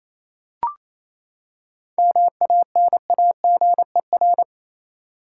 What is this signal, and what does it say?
Morse code, heard as a keyed tone, about 28 wpm, 700 Hz. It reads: MANAGER
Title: Seven Letter Words - 28wpm